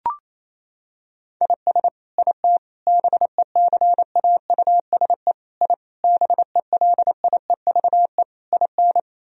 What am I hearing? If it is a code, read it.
Morse code, heard as a keyed tone, about 27 wpm, 700 Hz. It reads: IS IT BECAUSE I BELIEVE IN
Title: Sentences from Top 1000 Words - Review Part 3 - 28wpm